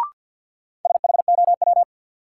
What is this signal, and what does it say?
Morse code, heard as a keyed tone, about 52 wpm, 700 Hz. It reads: SHOW
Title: Sets of 5 Words - New Word Drill - 50wpm